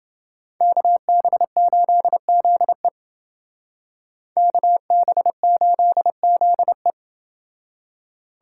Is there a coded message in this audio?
Yes: Morse code from 0.6 to 6.9 s, heard as a keyed tone, about 29 wpm, 700 Hz. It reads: KB8ZE KB8ZE